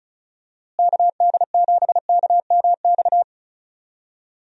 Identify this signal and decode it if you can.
Morse code, heard as a keyed tone, about 36 wpm, 700 Hz. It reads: KD7KMX
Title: USA Call Signs - 6 Characters - Repeated 2x - 35wpm